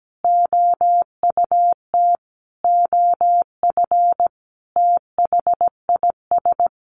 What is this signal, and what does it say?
Morse code, heard as a keyed tone, about 17 wpm, 700 Hz. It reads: OUT OF THIS
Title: Sentences from Top 1200 Words 17wpm